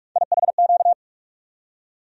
Morse code, heard as a keyed tone, about 41 wpm, 700 Hz. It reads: IHQ
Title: USA Call Sign Postfixes - 3 Letters 45wpm